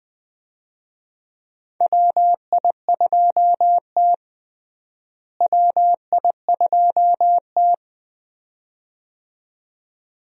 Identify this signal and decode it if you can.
Morse code, heard as a keyed tone, about 20 wpm, 700 Hz. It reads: WI2T WI2T
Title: USA Call Signs - 4 Characters - Repeated 2x - 20wpm